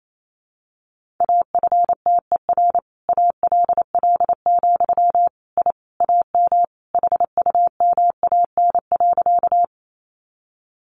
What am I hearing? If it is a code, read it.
Morse code, heard as a keyed tone, about 28 wpm, 700 Hz. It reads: AFTER ALL, I AM HUMAN.
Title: Sentences from Top 600 Words 28wpm